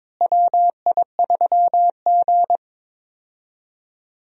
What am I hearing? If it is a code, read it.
Morse code, heard as a keyed tone, about 21 wpm, 700 Hz. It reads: WI3G
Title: USA Call Signs - 4 Characters - Repeated 3x - 22wpm